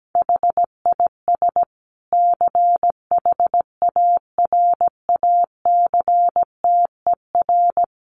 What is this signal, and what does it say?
Morse code, heard as a keyed tone, about 17 wpm, 700 Hz. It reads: HIS CHARACTER
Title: Sentences from Top 1300 Words 17wpm